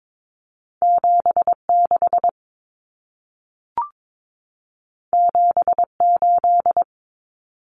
Morse code, heard as a keyed tone, about 22 wpm, 700 Hz. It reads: 76 78
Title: Numbers - 2-Digit 22wpm